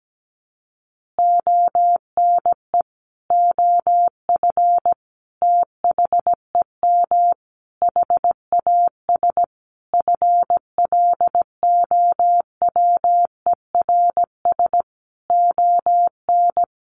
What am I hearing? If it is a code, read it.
Morse code, heard as a keyed tone, about 17 wpm, 700 Hz. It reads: ONE OF THEM HAS FLOWERS ON
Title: Additional Easy Phrases 17wpm